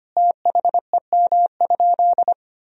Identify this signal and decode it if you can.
Morse code, heard as a keyed tone, about 25 wpm, 700 Hz. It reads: THEM?
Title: Sentences from Top 1000 Words - Review Part 3 - 25wpm